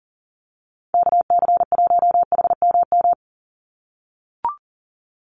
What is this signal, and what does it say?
Morse code, heard as a keyed tone, about 40 wpm, 700 Hz. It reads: KC1HMM
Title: USA Call Signs 40wpm